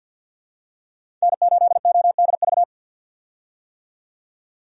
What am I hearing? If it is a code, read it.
Morse code, heard as a keyed tone, about 48 wpm, 700 Hz. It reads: N8ODV
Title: USA Call Signs - 5 Characters - 50wpm